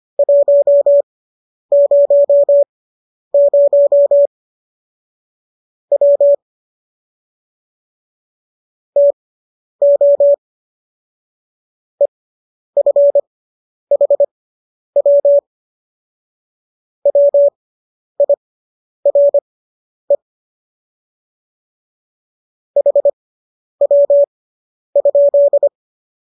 Morse code, heard as a keyed tone, about 25 wpm, spaced as for 10 wpm, 570 Hz. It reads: 100 W TO EFHW WIRE HW?